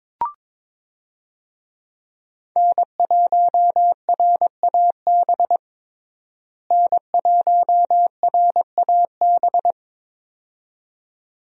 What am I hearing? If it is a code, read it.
Morse code, heard as a keyed tone, about 22 wpm, 700 Hz. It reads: N1RAB N1RAB
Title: USA Call Signs - 5 Characters - Repeated 2x - 22wpm